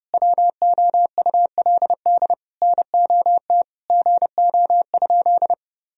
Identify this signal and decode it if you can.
Morse code, heard as a keyed tone, about 30 wpm, 700 Hz. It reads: WOULD NOT GO?
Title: Sentences from Top 1000 Words - Review Part 4 - 30wpm